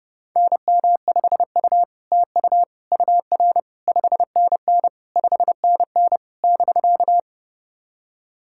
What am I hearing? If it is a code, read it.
Morse code, heard as a keyed tone, about 30 wpm, 700 Hz. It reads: NM5U TU UR 5NN 5NN <BK>
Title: SOTA QSO Practice 30wpm